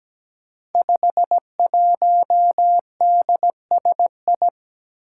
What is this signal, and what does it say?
Morse code, heard as a keyed tone, about 17 wpm, 700 Hz. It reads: 51DSI
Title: Five Challenging Letter-Number Combinations 17wpm